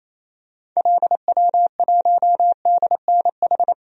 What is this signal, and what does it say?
Morse code, heard as a keyed tone, about 28 wpm, 700 Hz. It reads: LW1DNH
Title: DXCC-100 Call Signs - 6 Characters - Repeated 2x - 28wpm